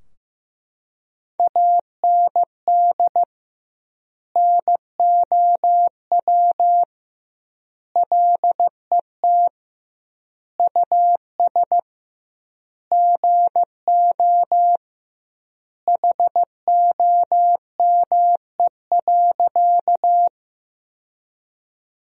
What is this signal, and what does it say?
Morse code, heard as a keyed tone, about 15 wpm, 700 Hz. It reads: AND NOW LET US GO HOME.
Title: 2x Word-Spacing - Sentences from Top 300 Words - 15wpm